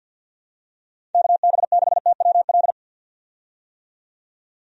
Morse code, heard as a keyed tone, about 51 wpm, 700 Hz. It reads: KB6TWL
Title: USA Call Signs - 6 Characters - 50wpm